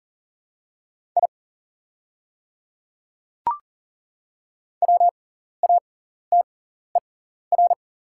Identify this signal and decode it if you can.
Morse code, heard as a keyed tone, about 40 wpm, spaced as for 14 wpm, 690 Hz. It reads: I WATER
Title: 6x Character Spacing - Top 100 Words - 40wpm